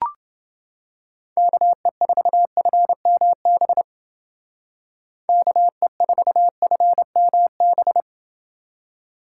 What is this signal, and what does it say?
Morse code, heard as a keyed tone, about 29 wpm, 700 Hz. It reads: KE4FMB KE4FMB